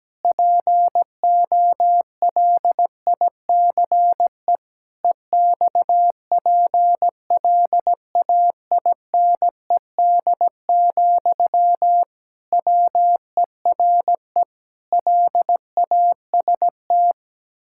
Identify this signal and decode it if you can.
Morse code, heard as a keyed tone, about 17 wpm, 700 Hz. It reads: POLICE EXPLAINED, WERE LAST
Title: Pangram Challenge v2 - 17wpm